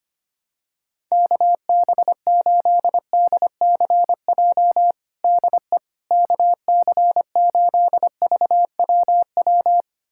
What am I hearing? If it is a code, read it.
Morse code, heard as a keyed tone, about 25 wpm, 700 Hz. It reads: KB8DCJ DE KC8VWW
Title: USA Callsign DE Callsign - 6 Characters - 25wpm